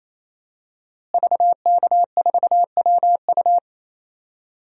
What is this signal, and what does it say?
Morse code, heard as a keyed tone, about 28 wpm, 700 Hz. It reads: VK4WU
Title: DXCC-100 Call Signs - 5 Characters - Repeated 3x - 28wpm